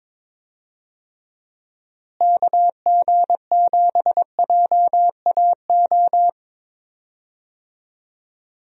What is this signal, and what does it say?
Morse code, heard as a keyed tone, about 22 wpm, 700 Hz. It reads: KG7JAO
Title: USA Call Signs - 6 Characters - Repeated 3x - 22wpm